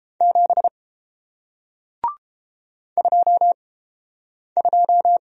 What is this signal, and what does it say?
Morse code, heard as a keyed tone, about 32 wpm, spaced as for 8 wpm, 700 Hz. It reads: 7 22